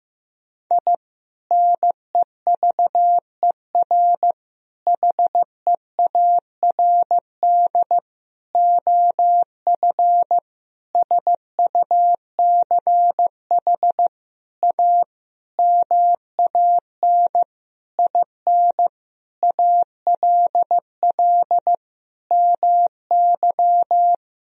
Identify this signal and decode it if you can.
Morse code, heard as a keyed tone, about 15 wpm, 700 Hz. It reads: I NEVER HEARD OF SUCH A MAN IN ALL MY
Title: Sentences from Top 1000 Words - Review Part 3 - 15wpm